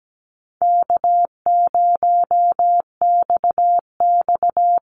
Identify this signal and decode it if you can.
Morse code, heard as a keyed tone, about 17 wpm, 700 Hz. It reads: K0XX
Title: Easy Call Signs - 17wpm